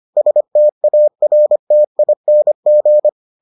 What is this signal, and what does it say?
Morse code, heard as a keyed tone, about 25 wpm, 600 Hz. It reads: STARTING